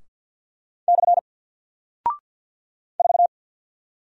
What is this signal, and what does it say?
Morse code, heard as a keyed tone, about 50 wpm, 700 Hz. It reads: / 4
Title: Single Character (Letter:Number:Punctuation:Prosign) 50wpm